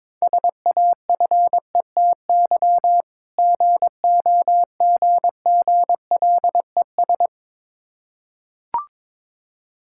Morse code, heard as a keyed tone, about 22 wpm, 700 Hz. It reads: SAFETY GOGGLES
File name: Difficult Words with Hints - 22wpm